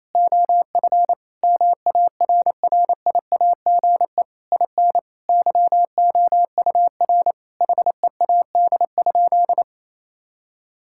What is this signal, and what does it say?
Morse code, heard as a keyed tone, about 28 wpm, 700 Hz. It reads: OF MARRIAGE IN YOUR HEAD?
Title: Sentences from Top 1000 Words - Review Part 5 - 28wpm